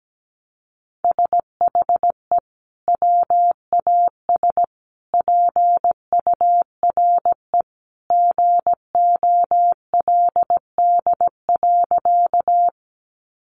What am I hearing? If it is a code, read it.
Morse code, heard as a keyed tone, about 17 wpm, 700 Hz. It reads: SHE WAS PURE GOLD.
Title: Sentences from Top 1500 Words 17wpm